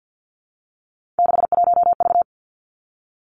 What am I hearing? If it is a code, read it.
Morse code, heard as a keyed tone, about 50 wpm, 700 Hz. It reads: 61V
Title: Three Challenging Letter-Number Combinations 50wpm